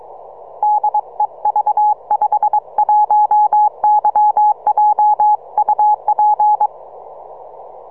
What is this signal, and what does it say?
Morse code, heard as a keyed tone, about 22 wpm, 840 Hz. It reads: DE V51YJ UP
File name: simple-file-list / V51YJ 6MCW